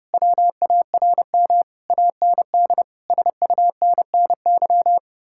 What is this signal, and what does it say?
Morse code, heard as a keyed tone, about 29 wpm, 700 Hz. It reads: WARM AND SUNNY